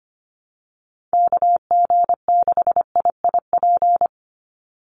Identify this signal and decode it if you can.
Morse code, heard as a keyed tone, about 25 wpm, 700 Hz. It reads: KG6IIP
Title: Call Signs - 25wpm